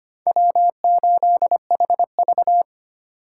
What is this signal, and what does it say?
Morse code, heard as a keyed tone, about 31 wpm, 700 Hz. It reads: W8HV
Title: Speed Racing - USA Call Signs 25wpm